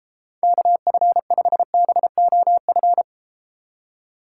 Morse code, heard as a keyed tone, about 33 wpm, 700 Hz. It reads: KF5BOF